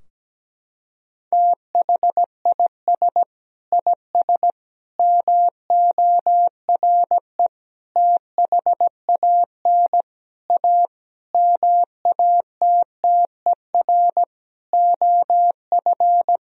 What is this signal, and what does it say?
Morse code, heard as a keyed tone, about 17 wpm, 700 Hz. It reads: THIS IS MORE THAN A MATTER OF
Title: Sentences from Top 1000 Words - Review Part 3 - 17wpm